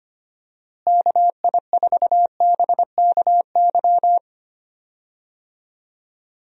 Morse code, heard as a keyed tone, about 25 wpm, 700 Hz. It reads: KI4BKY
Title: USA Call Signs - 6 Characters - Repeated 2x - 25wpm